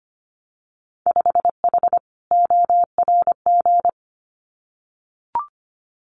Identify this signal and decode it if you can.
Morse code, heard as a keyed tone, about 25 wpm, 700 Hz. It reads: 5H ORG
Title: Winter Field Day Exchanges Encore 25wpm